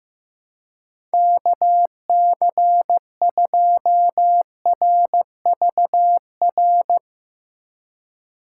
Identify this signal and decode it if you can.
Morse code, heard as a keyed tone, about 16 wpm, 700 Hz. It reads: KC2RVR
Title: Speed Racing - USA Call Signs 15wpm